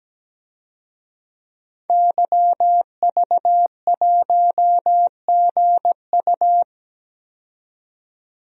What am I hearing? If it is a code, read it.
Morse code, heard as a keyed tone, about 17 wpm, 700 Hz. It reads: YV1GU
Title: DXCC-100 Call Signs - 5 Characters - 17wpm